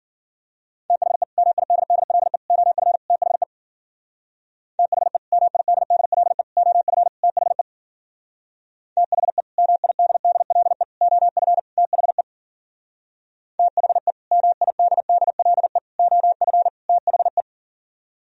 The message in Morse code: THE MIDDLE OF THE THE MIDDLE OF THE THE MIDDLE OF THE THE MIDDLE OF THE

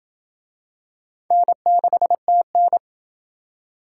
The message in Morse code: N6TN

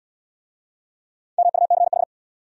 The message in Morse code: MUCH

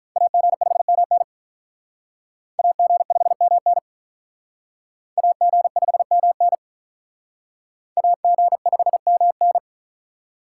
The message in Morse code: AG5MN AG5MN AG5MN AG5MN

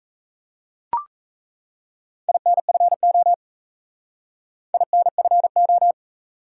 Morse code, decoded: INFO INFO

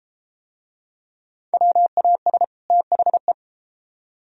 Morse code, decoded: WAS THE

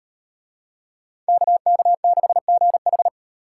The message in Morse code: KK6GH

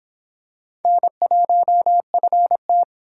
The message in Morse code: N1FT

